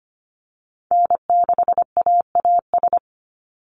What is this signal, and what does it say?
Morse code, heard as a keyed tone, about 25 wpm, 700 Hz. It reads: N6AAS
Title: USA Call Signs 25wpm